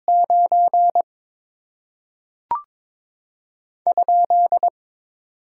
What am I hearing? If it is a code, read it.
Morse code, heard as a keyed tone, about 22 wpm, 700 Hz. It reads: 9 ?
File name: Top 90 QSO Elements - 6x Character Spacing - 22wpm